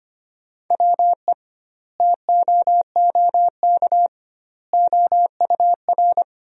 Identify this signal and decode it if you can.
Morse code, heard as a keyed tone, about 25 wpm, 700 Hz. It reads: WE TOOK OUR
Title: 2x Word-Spacing - Sentences from Top 400 Words - 25wpm